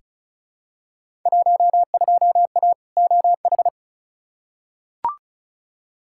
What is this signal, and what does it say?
Morse code, heard as a keyed tone, about 34 wpm, 700 Hz. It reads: 12A OH
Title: Field Day Exchanges Encore - 35wpm